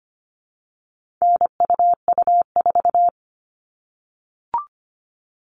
Morse code, heard as a keyed tone, about 25 wpm, 700 Hz. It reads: NUU4
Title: All Cut Numbers - 4-Digit 25wpm